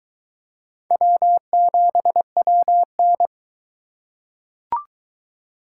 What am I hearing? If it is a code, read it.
Morse code, heard as a keyed tone, about 22 wpm, 700 Hz. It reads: W7WN